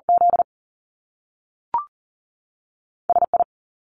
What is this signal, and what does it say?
Morse code, heard as a keyed tone, about 39 wpm, spaced as for 3 wpm, 700 Hz. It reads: ZS I
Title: Top Two Letter Combinations in Words 40wpm